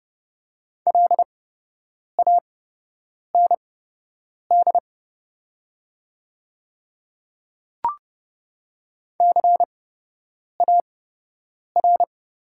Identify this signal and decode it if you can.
Morse code, heard as a keyed tone, about 30 wpm, spaced as for 8 wpm, 700 Hz. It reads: LAND CAR
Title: 8x Character Spacing - Top 300 Words - 30wpm